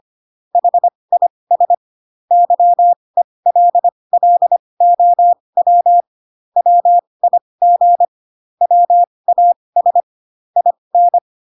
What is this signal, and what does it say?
Morse code, heard as a keyed tone, about 25 wpm, spaced as for 20 wpm, 700 Hz. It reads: HIS YELLOW WIG WAS IN